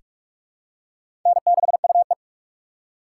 Morse code, heard as a keyed tone, about 46 wpm, 700 Hz. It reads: N6UE
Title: USA Call Signs - 4 Characters - 3x Character Spacing - 45wpm